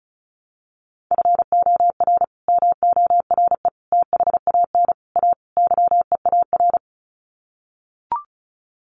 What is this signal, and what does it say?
Morse code, heard as a keyed tone, about 34 wpm, 700 Hz. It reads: FOR MORE THAN A YEAR
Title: Sets of 5 Words 35wpm